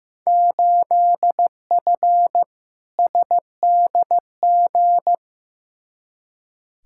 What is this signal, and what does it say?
Morse code, heard as a keyed tone, about 15 wpm, 700 Hz. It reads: 8F SDG
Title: Field Day Exchanges Encore - 15wpm